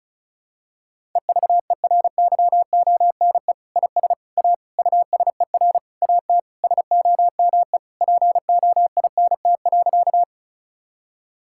Morse code, heard as a keyed tone, about 36 wpm, 700 Hz. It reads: EVERYONE IS A USER AT SOME POINT.
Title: Sentences from Top 1000 Words - Review Part 5 - 35wpm